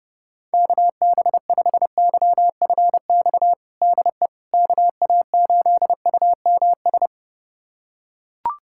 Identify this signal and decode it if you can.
Morse code, heard as a keyed tone, about 31 wpm, 700 Hz. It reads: KB5YFX DE KA8UMS
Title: USA Callsign DE Callsign - 6 Characters - 30wpm